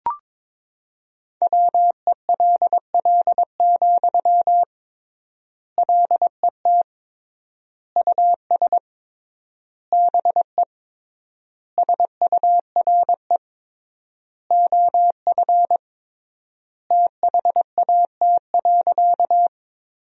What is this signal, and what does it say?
Morse code, heard as a keyed tone, about 22 wpm, 700 Hz. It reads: WELL, LET US BE SURE OF THAT.
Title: 3x Word-Spacing - Sentences from Top 300 Words - 22wpm